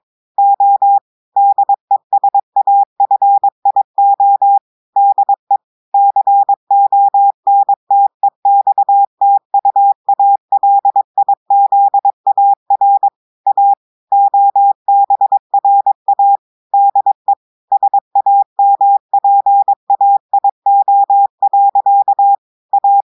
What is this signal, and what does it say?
Morse code, heard as a keyed tone, about 22 wpm, 800 Hz. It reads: O DESAFIO DE CONTEXTUALIZAR A OBRA DE SAMPAIO. A
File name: audios / portugais / portugais texte en clair 1 vitesse 22 mots minutes